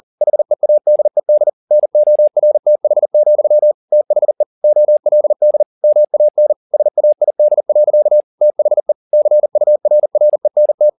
Morse code, heard as a keyed tone, about 40 wpm, 600 Hz. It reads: HEADED NORTH, THE OLD MAN SAID. THE CURRENT